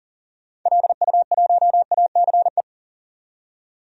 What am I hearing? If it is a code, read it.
Morse code, heard as a keyed tone, about 40 wpm, 700 Hz. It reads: LU1ACE
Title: DXCC-100 Call Signs - 6 Characters - 40wpm